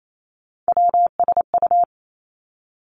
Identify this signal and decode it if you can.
Morse code, heard as a keyed tone, about 28 wpm, 700 Hz. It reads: WSU
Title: Three Challenging Letter-Number Combinations 28wpm